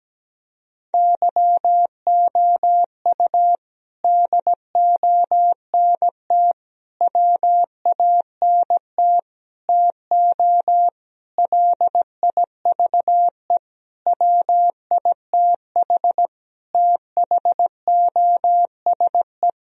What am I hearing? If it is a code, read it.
Morse code, heard as a keyed tone, about 17 wpm, 700 Hz. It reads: YOU DONT WANT TO LIVE WITH THOSE
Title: Sentences from Top 1000 Words - Review Part 1 - 17wpm